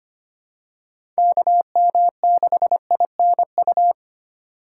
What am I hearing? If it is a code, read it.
Morse code, heard as a keyed tone, about 25 wpm, 700 Hz. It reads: KM6INU
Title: USA Call Signs - 6 Characters - 2x Character Spacing - 25wpm